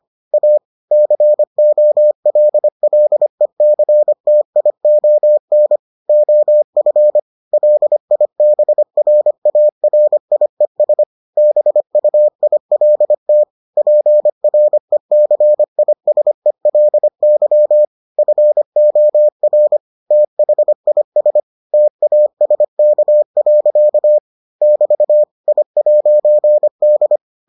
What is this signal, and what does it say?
Morse code, heard as a keyed tone, about 25 wpm, 600 Hz. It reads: A COLLECTION OF LIBRARIES BUILT PRECISELY FOR THIS TASK. <BT> I'D